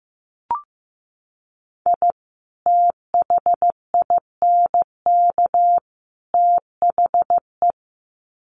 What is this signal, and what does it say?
Morse code, heard as a keyed tone, about 15 wpm, 700 Hz. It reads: I THINK THE
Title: Sets of 3 Words 15wpm